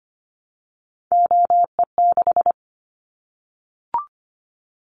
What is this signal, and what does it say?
Morse code, heard as a keyed tone, about 25 wpm, 700 Hz. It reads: OE6
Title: Callsign Prefixes for Top 50 DXCC - 25wpm